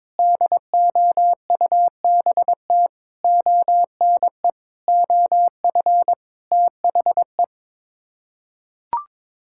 Morse code, heard as a keyed tone, about 22 wpm, 700 Hz. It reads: DOUBT ONE OF THE
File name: practice / 22wpm / Sets of 7 Words - 22wpm